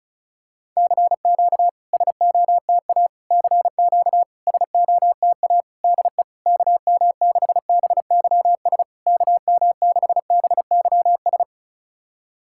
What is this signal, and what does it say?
Morse code, heard as a keyed tone, about 34 wpm, 660 Hz. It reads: CQ SOTA CQ SOTA DE KM6BYS KM6BYS
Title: SOTA QSO Practice 35wpm